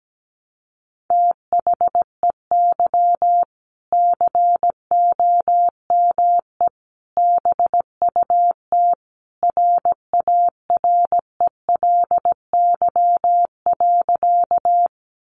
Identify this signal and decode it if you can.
Morse code, heard as a keyed tone, about 17 wpm, 700 Hz. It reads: THEY COME BUT RARELY.
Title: Sentences from Top 1400 Words 17wpm